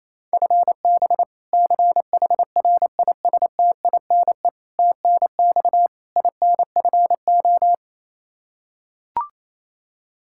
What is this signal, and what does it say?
Morse code, heard as a keyed tone, about 28 wpm, 700 Hz. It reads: FB CHRISTINE TNX INFO
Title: Practice QSO - Exchange 3 - More Info - 28wpm